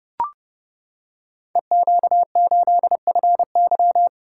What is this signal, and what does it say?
Morse code, heard as a keyed tone, about 30 wpm, 700 Hz. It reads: EQ8FY
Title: Code Groups - 5 Letter:Numbers - 30wpm